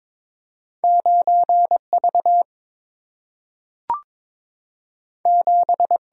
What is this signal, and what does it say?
Morse code, heard as a keyed tone, about 22 wpm, 690 Hz. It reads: 9 V 7
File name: practice / 22wpm / Two Letter-Number Combinations - 22wpm